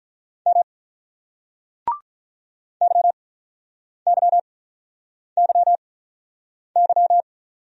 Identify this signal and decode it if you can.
Morse code, heard as a keyed tone, about 50 wpm, spaced as for 9 wpm, 700 Hz. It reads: M YYYY